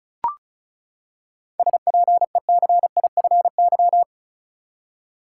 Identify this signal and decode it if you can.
Morse code, heard as a keyed tone, about 34 wpm, 700 Hz. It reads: SPECIFY
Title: Seven Letter Words - 35wpm